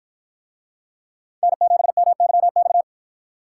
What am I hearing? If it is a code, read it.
Morse code, heard as a keyed tone, about 53 wpm, 700 Hz. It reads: N7MYX